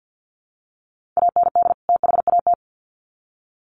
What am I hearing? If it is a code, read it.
Morse code, heard as a keyed tone, about 50 wpm, 700 Hz. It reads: AND THAT
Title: Sets of 2 Words 50wpm